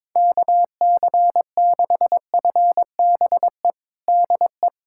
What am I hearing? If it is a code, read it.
Morse code, heard as a keyed tone, about 22 wpm, 700 Hz. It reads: KC6FBE DE
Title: USA Callsign DE Callsign - 6 Characters - 22wpm